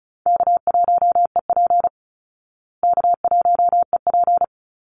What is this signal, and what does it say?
Morse code, heard as a keyed tone, about 34 wpm, 700 Hz. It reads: K1EP K1EP
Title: Easy Call Signs Repeated 2x - 35wpm